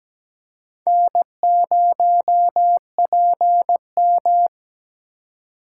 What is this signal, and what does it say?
Morse code, heard as a keyed tone, about 17 wpm, 700 Hz. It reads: N0PM
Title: USA Call Signs - 4x Character Spacing - 17wpm